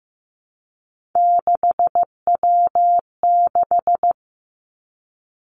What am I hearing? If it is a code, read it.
Morse code, heard as a keyed tone, about 15 wpm, 700 Hz. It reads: 6W6
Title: Three Challenging Letter-Number Combinations 15wpm